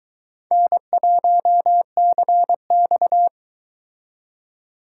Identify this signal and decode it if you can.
Morse code, heard as a keyed tone, about 22 wpm, 700 Hz. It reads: N1CX